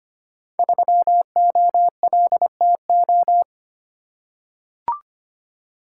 Morse code, heard as a keyed tone, about 25 wpm, 700 Hz. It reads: 3OLTO
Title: Code Groups - 5 Letter:Numbers - 25wpm